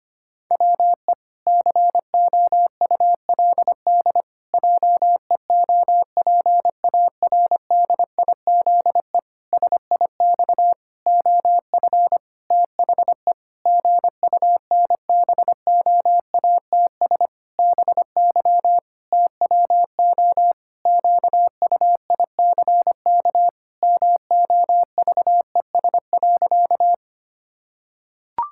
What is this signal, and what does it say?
Morse code, heard as a keyed tone, about 25 wpm, 700 Hz. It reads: WE COULD JEOPARDIZE SIX OF THE GUNBOATS BY TWO QUICK MOVES.
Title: Pangram Challenge v2 - 25wpm